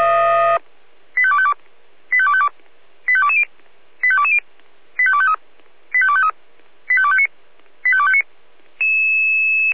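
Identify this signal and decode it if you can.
Five-tone selective call (ZVEI), heard as quick runs of five tones, tones of about 70 ms. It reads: 85252 85252 85220 85220 85253 85253 85259 85259
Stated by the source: Received by SDR; ZVEI five-tone call 85264, channel 496.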